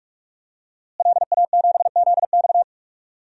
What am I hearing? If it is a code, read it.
Morse code, heard as a keyed tone, about 46 wpm, 700 Hz. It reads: LA7ZX
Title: DXCC-100 Call Signs - 5 Characters - Repeated 2x - 45wpm